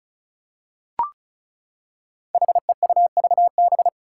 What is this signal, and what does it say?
Morse code, heard as a keyed tone, about 34 wpm, 700 Hz. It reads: HEUVB
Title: Five Challenging Letter-Number Combinations 35wpm